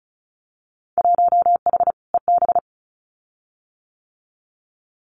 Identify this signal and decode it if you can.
Morse code, heard as a keyed tone, about 34 wpm, 700 Hz. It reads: 1H EB
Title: Winter Field Day Exchanges Encore 35wpm